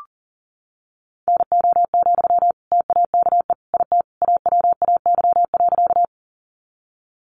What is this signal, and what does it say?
Morse code, heard as a keyed tone, about 40 wpm, 700 Hz. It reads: NO, TAKE IT AWAY.
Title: Sentences from Top 600 Words 40wpm